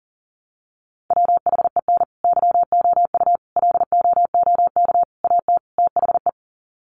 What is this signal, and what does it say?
Morse code, heard as a keyed tone, about 40 wpm, 700 Hz. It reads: WHEN YOU LOOK AT THE
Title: Sets of 5 Words 40wpm